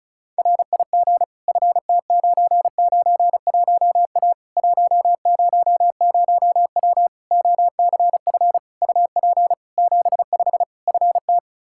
Morse code, heard as a keyed tone, about 36 wpm, 700 Hz. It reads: RIG FT991A 100W OCF UP 75 FT
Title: K1USN SST QSO Practice 35wpm